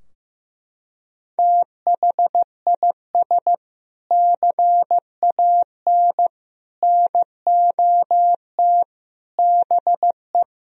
Morse code, heard as a keyed tone, about 15 wpm, 700 Hz. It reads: THIS CAN NOT BE
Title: Sentences from Top 1000 Words - Review Part 3 - 15wpm